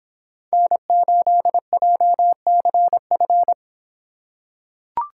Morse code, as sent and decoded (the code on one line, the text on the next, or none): -. ---.. .--- -.-. ..-.
N8JCF